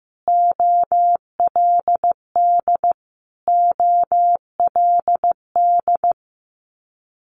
Morse code, heard as a keyed tone, about 15 wpm, 700 Hz. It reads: OLD OLD
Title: Multi-letter Phonemes Repeated 3x 15wpm